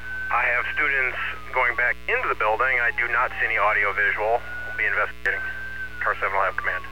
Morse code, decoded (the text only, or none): none